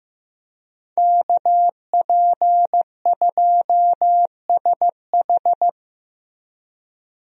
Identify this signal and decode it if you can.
Morse code, heard as a keyed tone, about 15 wpm, 700 Hz. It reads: KP2SH
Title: DXCC-100 Call Signs - 5 Characters - Repeated 2x - 15wpm